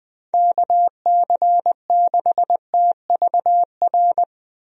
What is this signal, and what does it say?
Morse code, heard as a keyed tone, about 20 wpm, 700 Hz. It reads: KC6TVR
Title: USA Call Signs - 6 Characters - Repeated 3x - 20wpm